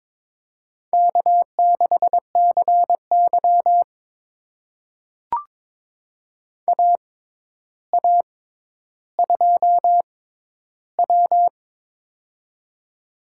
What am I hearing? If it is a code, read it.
Morse code, heard as a keyed tone, about 22 wpm, spaced as for 11 wpm, 700 Hz. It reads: K6CY AA2W
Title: USA Call Signs - 4 Characters - 6x Character Spacing - 22wpm